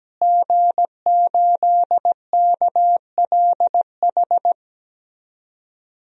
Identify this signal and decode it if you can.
Morse code, heard as a keyed tone, about 17 wpm, 700 Hz. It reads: G8KLH
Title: DXCC-100 Call Signs - 5 Characters - Repeated 3x - 17wpm